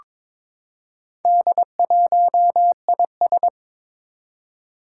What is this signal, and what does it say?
Morse code, heard as a keyed tone, about 22 wpm, 700 Hz. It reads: D1IS
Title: Four Challenging Letter-Number Combinations 22wpm